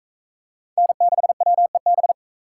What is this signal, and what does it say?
Morse code, heard as a keyed tone, about 42 wpm, 700 Hz. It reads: N6WEB